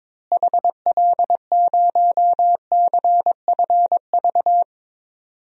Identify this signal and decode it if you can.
Morse code, heard as a keyed tone, about 22 wpm, 700 Hz. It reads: HL0CFV
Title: DXCC-100 Call Signs - 6 Characters - Repeated 2x - 22wpm